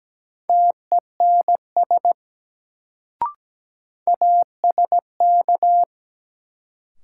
Morse code, heard as a keyed tone, about 17 wpm, 700 Hz. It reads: TENS ASK
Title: Sets of 4 Words - New Word Drill - 17wpm